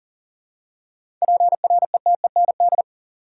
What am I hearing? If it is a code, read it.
Morse code, heard as a keyed tone, about 40 wpm, 700 Hz. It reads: PRETEND
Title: Seven Letter Words - 40wpm